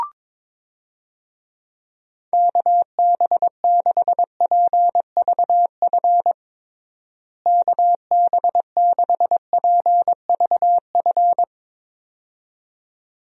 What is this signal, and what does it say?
Morse code, heard as a keyed tone, about 22 wpm, 700 Hz. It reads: KB6PVF KB6PVF
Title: USA Call Signs - 6 Characters - Repeated 2x - 22wpm